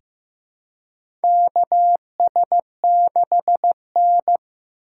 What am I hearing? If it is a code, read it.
Morse code, heard as a keyed tone, about 15 wpm, 700 Hz. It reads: KS6N
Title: USA Call Signs - 4 Characters - 8x Character Spacing - 15wpm